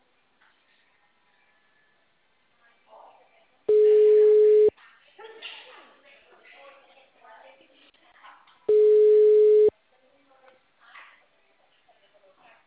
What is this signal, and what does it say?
Ringback tone (Europe, 425 Hz), with speech.
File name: out- -2007- - - .3301